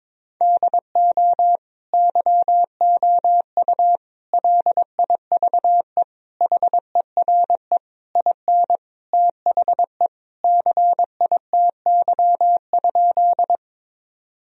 Morse code, heard as a keyed tone, about 21 wpm, 700 Hz. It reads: DO YOU LIVE HERE IN THE CITY?
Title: Sentences from Top 1000 Words - Review Part 1 - 22wpm